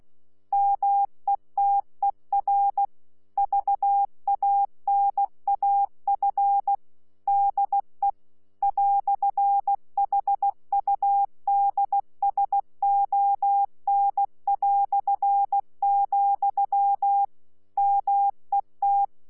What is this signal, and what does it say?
Morse code, heard as a keyed tone, about 16 wpm, 800 Hz. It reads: METER VANAF DE "HUDSON", MET
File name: Z PI4VRZ / PI4VRZ 14-jun-2025